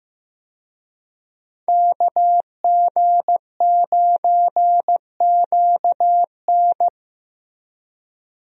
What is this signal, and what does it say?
Morse code, heard as a keyed tone, about 16 wpm, 700 Hz. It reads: KG9QN
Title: USA Call Signs - 5 Characters - Repeated 2x - 15wpm